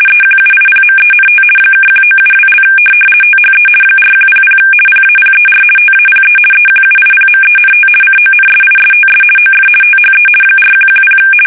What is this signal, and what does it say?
RTTY, about 50 baud. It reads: EQUIPMENT (1972) THE 4100 PAPER TAPE EQUIPMENT CONSISTS OF THE 4110
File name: fsk850-50B / teletype-corp